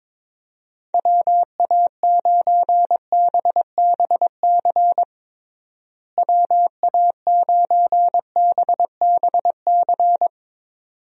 Morse code, heard as a keyed tone, about 22 wpm, 700 Hz. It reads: WA9BBC WA9BBC
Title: USA Call Signs - 6 Characters - Repeated 2x - 22wpm